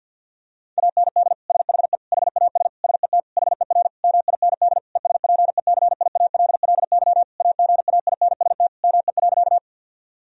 Morse code, heard as a keyed tone, about 52 wpm, 700 Hz. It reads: AND SHE HAS SET HER MIND ESPECIALLY AGAINST ME.
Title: Sentences from Top 1000 Words - Review Part 4 - 50wpm